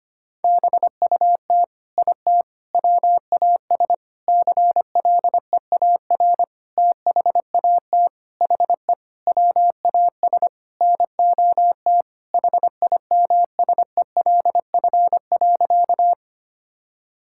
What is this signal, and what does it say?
Morse code, heard as a keyed tone, about 25 wpm, 700 Hz. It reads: BUT IT WAS CLEAR THAT HE WAS NOT HIMSELF.
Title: Sentences from Top 1600 Words 25wpm